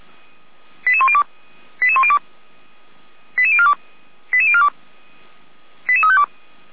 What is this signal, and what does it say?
Five-tone selective call (ZVEI), heard as quick runs of five tones, tones of about 70 ms. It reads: 88102 88102 88042 88042 88352
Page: /455/SDR/ZVEI/89501/2024.04.18/